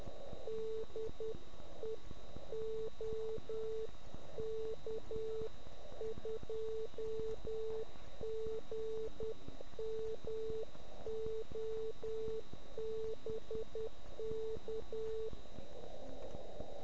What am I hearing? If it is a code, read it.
Morse code, heard as a keyed tone, about 10 wpm, 440 Hz. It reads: DE OK2GMO BK